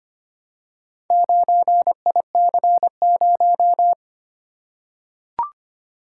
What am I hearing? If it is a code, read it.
Morse code, heard as a keyed tone, about 25 wpm, 700 Hz. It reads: 9IC0
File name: practice / 25wpm / Four Letter-Number Combinations - 25wpm